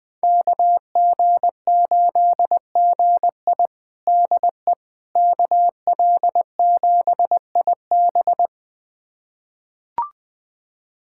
Morse code, heard as a keyed tone, about 20 wpm, 700 Hz. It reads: KG8GI DE KL7IB
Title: USA Callsign DE Callsign 20wpm